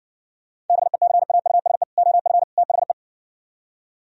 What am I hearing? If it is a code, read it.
Morse code, heard as a keyed tone, about 58 wpm, 700 Hz. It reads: BECAUSE OF THE